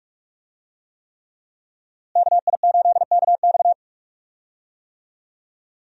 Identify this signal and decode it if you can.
Morse code, heard as a keyed tone, about 44 wpm, 700 Hz. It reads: KI8KX
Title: USA Call Signs - 5 Characters - Repeated 3x - 45wpm